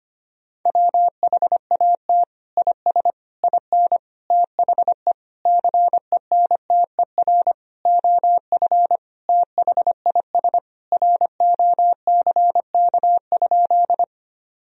Morse code, heard as a keyed tone, about 25 wpm, 700 Hz. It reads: WHAT IS IN THE CENTER OF THIS ROCK?
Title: Sentences from Top 1000 Words - Review Part 2 - 25wpm